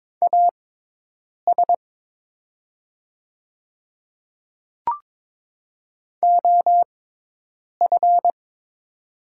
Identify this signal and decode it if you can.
Morse code, heard as a keyed tone, about 22 wpm, spaced as for 8 wpm, 700 Hz. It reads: AS OF
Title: 6x Character Spacing - Top 100 Words - 22wpm